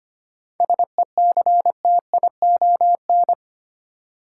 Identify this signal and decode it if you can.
Morse code, heard as a keyed tone, about 25 wpm, 700 Hz. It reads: SECTION
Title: Sentences from Top 1000 Words - Review Part 5 - New Word Drill - 25wpm